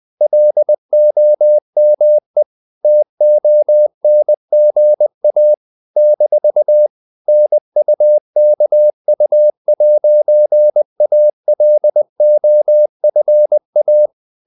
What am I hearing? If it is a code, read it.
Morse code, heard as a keyed tone, about 20 wpm, 600 Hz. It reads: LOME TONGA - NUKU'ALOFA
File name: CW / Practice / NationalCapitals20